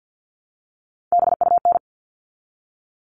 Morse code, heard as a keyed tone, about 48 wpm, 700 Hz. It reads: BUN
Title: All Cut Numbers - 3-Digit 50wpm